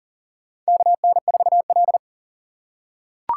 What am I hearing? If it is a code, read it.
Morse code, heard as a keyed tone, about 40 wpm, 700 Hz. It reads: KN4L
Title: USA Call Signs - 4 Characters - 4x Character Spacing - 40wpm